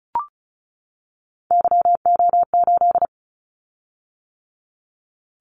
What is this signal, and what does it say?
Morse code, heard as a keyed tone, about 36 wpm, 700 Hz. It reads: YO8
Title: Callsign Prefixes for Top 25 DXCC - 35wpm